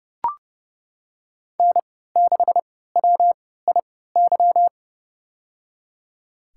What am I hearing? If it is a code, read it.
Morse code, heard as a keyed tone, about 32 wpm, spaced as for 17 wpm, 700 Hz. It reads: N6WIY
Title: USA Call Signs - 5 Characters - 3x Character Spacing - 30wpm